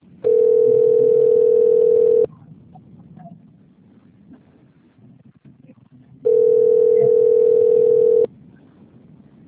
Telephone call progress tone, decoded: ringback tone

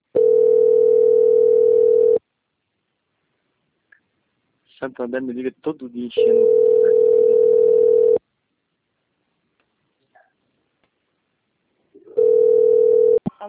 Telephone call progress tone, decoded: ringback tone

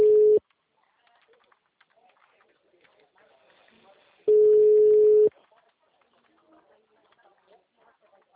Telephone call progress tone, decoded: ringback tone

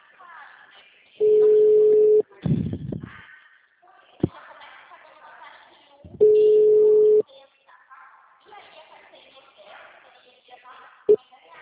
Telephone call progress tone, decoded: ringback tone

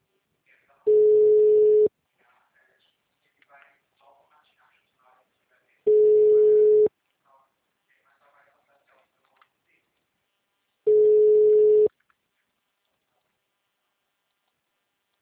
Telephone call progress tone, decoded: ringback tone

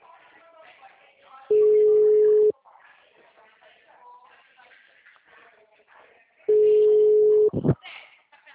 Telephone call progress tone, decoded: ringback tone

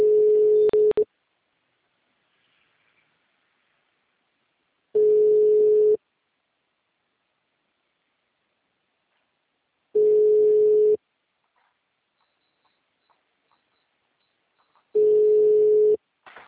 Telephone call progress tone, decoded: ringback tone

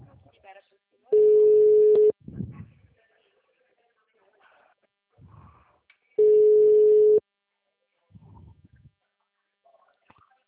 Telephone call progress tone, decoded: ringback tone

